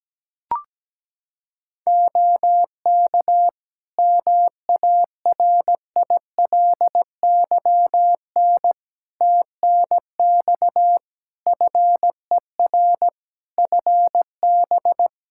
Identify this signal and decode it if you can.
Morse code, heard as a keyed tone, about 17 wpm, 700 Hz. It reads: OK MARILYN TNX FER FB
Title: Practice QSO - Exchange 4 - Ending - 17wpm